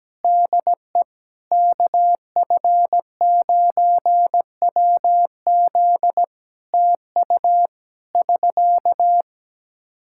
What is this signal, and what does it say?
Morse code, heard as a keyed tone, about 17 wpm, 700 Hz. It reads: DE KF9WZ TU <SK>
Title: K1USN SST QSO Practice 17wpm